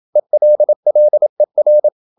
Morse code, heard as a keyed tone, about 27 wpm, 600 Hz. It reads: ELLER
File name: Ferie QRQ 27wpm